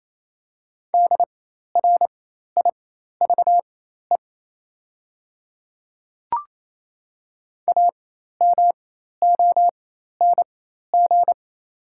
Morse code, heard as a keyed tone, about 28 wpm, spaced as for 13 wpm, 700 Hz. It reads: DRIVE AMONG
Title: 4x Character Spacing - Top 500 Words - 28wpm